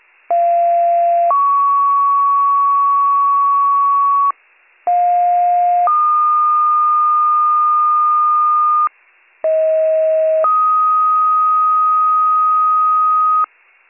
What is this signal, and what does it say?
Two-tone page, heard as a one-second tone, then a three-second tone, about 690 then 1120 Hz, about 710 then 1220 Hz, about 640 then 1250 Hz. It is heard from 0.3 to 13.5 s.